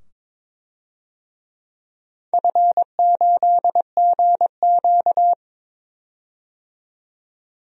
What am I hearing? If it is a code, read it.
Morse code, heard as a keyed tone, about 22 wpm, 700 Hz. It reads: F8GQ
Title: DXCC-100 Call Signs - 4 Characters - 22wpm